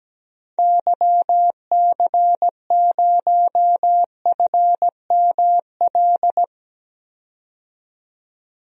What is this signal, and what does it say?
Morse code, heard as a keyed tone, about 17 wpm, 700 Hz. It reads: YC0FML
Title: DXCC-100 Call Signs - 6 Characters - 17wpm